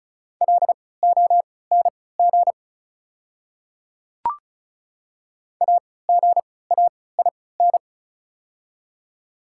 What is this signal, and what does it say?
Morse code, heard as a keyed tone, about 36 wpm, spaced as for 20 wpm, 700 Hz. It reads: LONG AGAIN
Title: 3x Character Spacing - Top 200 Words - 35wpm